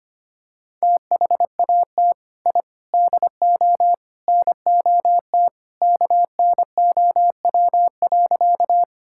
Morse code, heard as a keyed tone, about 25 wpm, 700 Hz. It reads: THAT I DO NOT KNOW.
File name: practice / 25wpm / Sentences with 4-Word Phrases - 25wpm